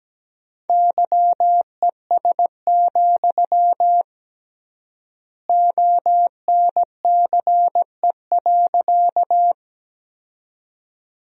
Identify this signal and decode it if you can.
Morse code, heard as a keyed tone, about 17 wpm, 700 Hz. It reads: YES, ONCE.
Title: 3x Word-Spacing - Sentences from Top 500 Words - 17wpm